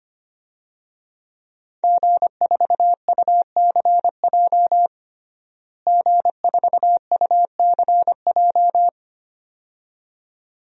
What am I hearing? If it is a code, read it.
Morse code, heard as a keyed tone, about 25 wpm, 700 Hz. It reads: G4UCJ G4UCJ
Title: DXCC-100 Call Signs - 5 Characters - Repeated 2x - 25wpm